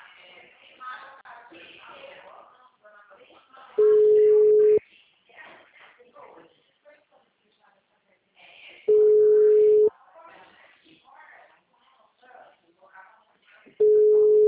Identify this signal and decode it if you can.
Ringback tone (Europe, 425 Hz), with speech.